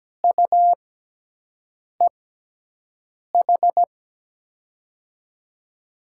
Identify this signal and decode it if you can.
Morse code, heard as a keyed tone, about 17 wpm, 700 Hz. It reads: U E H
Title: USA Call Signs - 6 Characters - 6x Character Spacing - 17wpm